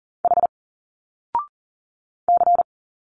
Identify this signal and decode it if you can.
Morse code, heard as a keyed tone, about 40 wpm, 700 Hz. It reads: H C
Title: Single Letters 40wpm